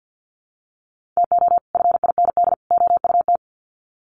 Morse code, heard as a keyed tone, about 48 wpm, 700 Hz. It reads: TO FIND OUT
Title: Sets of 3 Words 50wpm